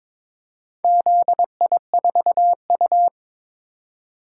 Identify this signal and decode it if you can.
Morse code, heard as a keyed tone, about 22 wpm, 700 Hz. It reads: ZI4U
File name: practice / 22wpm / Four Letter-Number Combinations - 22wpm